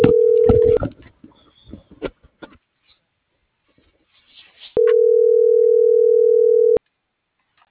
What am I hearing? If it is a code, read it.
Ringback tone (North America), 440 + 480 Hz.
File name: CENTRAL MULTIMODAL / out angel.velazquez - .40716